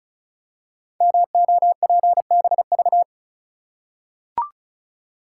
Morse code, heard as a keyed tone, about 36 wpm, 700 Hz. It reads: MOPBV
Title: Code Groups - 5 Letters - 35wpm